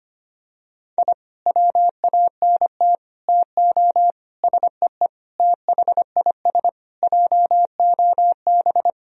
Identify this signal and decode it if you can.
Morse code, heard as a keyed tone, about 25 wpm, 700 Hz. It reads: I WANT TO SEE THIS JOB
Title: Sentences from Top 1000 Words - Review Part 3 - 25wpm